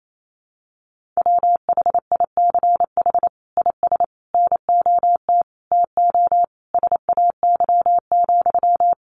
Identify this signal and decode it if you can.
Morse code, heard as a keyed tone, about 28 wpm, 700 Hz. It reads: WHICH IS NOT TO SAY,
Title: Sentences from Top 1400 Words 28wpm